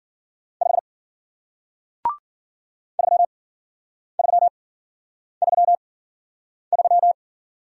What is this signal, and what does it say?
Morse code, heard as a keyed tone, about 53 wpm, spaced as for 9 wpm, 700 Hz. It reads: 5 3333